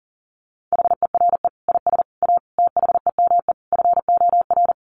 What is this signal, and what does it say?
Morse code, heard as a keyed tone, about 40 wpm, 700 Hz. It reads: HERE IS A THEME FOR
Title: Sentences from Top 1100 Words 40wpm